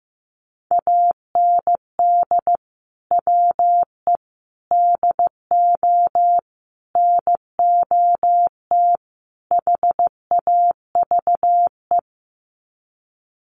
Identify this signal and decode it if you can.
Morse code, heard as a keyed tone, about 15 wpm, 700 Hz. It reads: AND WE DO NOT HAVE
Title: Sets of 5 Words Encore 15wpm